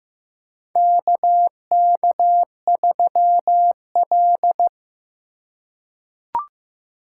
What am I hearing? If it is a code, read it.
Morse code, heard as a keyed tone, about 15 wpm, 700 Hz. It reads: KK3L
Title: USA Call Signs - 4 Characters - 3x Character Spacing - 15wpm